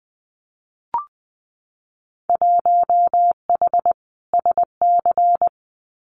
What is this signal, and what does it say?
Morse code, heard as a keyed tone, about 19 wpm, 700 Hz. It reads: 1H SC
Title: Winter Field Day Exchanges 20wpm